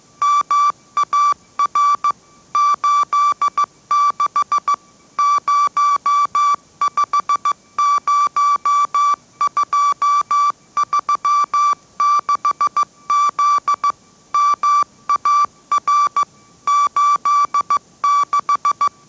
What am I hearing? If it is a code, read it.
Morse code, heard as a keyed tone, about 15 wpm, 1160 Hz. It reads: MAR 86 050236Z MAR 86
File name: mrx / mrx065 16wpm